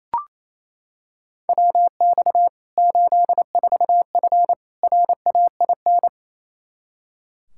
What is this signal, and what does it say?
Morse code, heard as a keyed tone, about 28 wpm, 700 Hz. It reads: WX 84F RAIN
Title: K1USN SST QSO Practice 28wpm